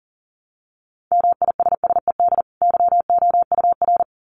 Morse code, heard as a keyed tone, about 40 wpm, 700 Hz. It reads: MISSED YOUR
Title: Common QSO Phrases 40wpm